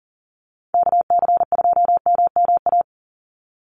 Morse code, heard as a keyed tone, about 40 wpm, 700 Hz. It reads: KC2MMA
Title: USA Call Signs 40wpm